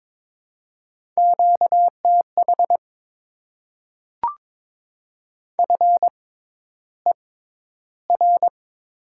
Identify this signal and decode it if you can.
Morse code, heard as a keyed tone, about 22 wpm, spaced as for 11 wpm, 690 Hz. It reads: QTH FER
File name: Top 20 QSO Elements - 6x Character Spacing - 22wpm